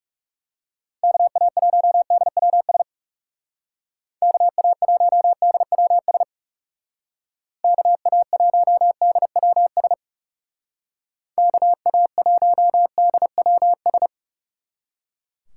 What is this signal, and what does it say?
Morse code, heard as a keyed tone, about 39 wpm, 700 Hz. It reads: KA1DWS KA1DWS KA1DWS KA1DWS